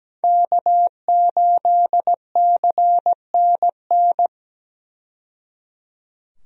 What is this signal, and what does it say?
Morse code, heard as a keyed tone, about 17 wpm, 700 Hz. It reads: K8CNN
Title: USA Call Signs - 5 Characters - 17wpm